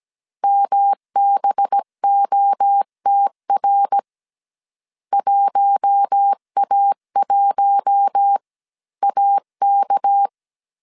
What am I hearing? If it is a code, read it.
Morse code, heard as a keyed tone, about 17 wpm, 800 Hz. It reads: MBOTR 1A1AK
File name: CW / DET / 1 6 1500C S C 16wpm 2733